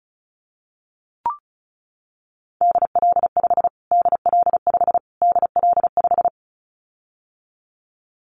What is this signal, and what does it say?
Morse code, heard as a keyed tone, about 34 wpm, 700 Hz. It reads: DL5 DL5 DL5
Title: Callsign Prefixes for Top 25 DXCC - Repeated 3x - 35wpm